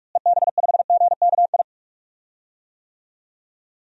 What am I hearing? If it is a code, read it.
Morse code, heard as a keyed tone, about 46 wpm, 700 Hz. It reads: EB5GKI
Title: DXCC-100 Call Signs - 6 Characters - Repeated 2x - 45wpm